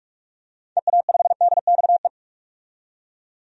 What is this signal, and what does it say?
Morse code, heard as a keyed tone, about 46 wpm, 700 Hz. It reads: EA5DXE
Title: DXCC-100 Call Signs - 6 Characters - 45wpm